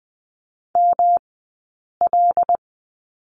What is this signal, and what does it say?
Morse code, heard as a keyed tone, about 20 wpm, spaced as for 8 wpm, 700 Hz. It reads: ML
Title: 217 Lesson 35d - Number 0 - Two Characters 20wpm